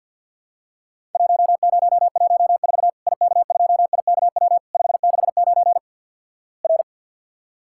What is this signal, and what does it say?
Morse code, heard as a keyed tone, about 51 wpm, 700 Hz. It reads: 1014 IK2IKW 569 R
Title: Worked All Europe DX Contest - QTC Exchange - 50wpm